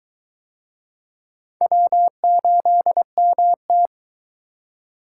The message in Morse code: W8MT

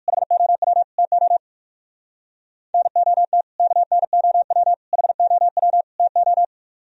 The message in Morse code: HOW TO NOT KNOW HOW TO

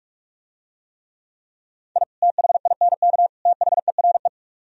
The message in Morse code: I THINK THERE

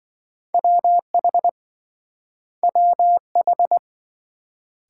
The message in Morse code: WH WH

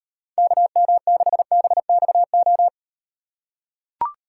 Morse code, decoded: KM6BXO

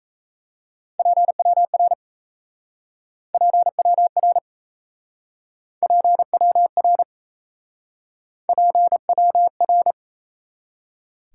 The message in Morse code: PWR PWR PWR PWR

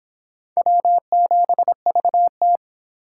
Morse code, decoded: W7VT

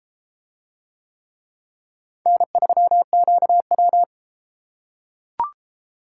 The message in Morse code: N3QW

